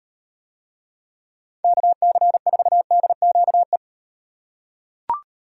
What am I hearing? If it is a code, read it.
Morse code, heard as a keyed tone, about 38 wpm, 700 Hz. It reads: KC4DQE